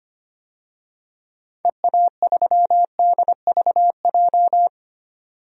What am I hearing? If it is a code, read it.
Morse code, heard as a keyed tone, about 25 wpm, 700 Hz. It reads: EA3DVJ
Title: DXCC-100 Call Signs - 6 Characters - 25wpm